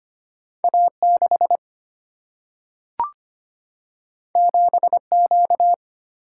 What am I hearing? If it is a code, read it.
Morse code, heard as a keyed tone, about 25 wpm, 700 Hz. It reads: A6 7Q
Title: Single-Letter Single-Number 25wpm